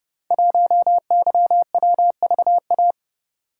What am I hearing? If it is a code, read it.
Morse code, heard as a keyed tone, about 30 wpm, 700 Hz. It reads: 1YWVA
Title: Code Groups - 5 Letter:Numbers - 30wpm